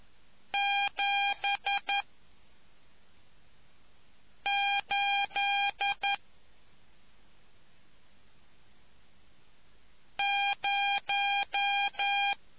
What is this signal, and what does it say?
Morse code, heard as a keyed tone, about 11 wpm, 800 Hz. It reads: 7 8 0